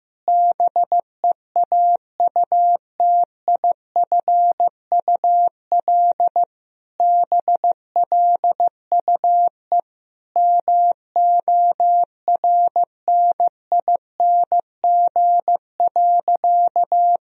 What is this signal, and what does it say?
Morse code, heard as a keyed tone, about 15 wpm, 700 Hz. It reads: BEAUTIFUL BLUE MORNING.
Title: Sentences from Top 1000 Words - Review Part 3 - 15wpm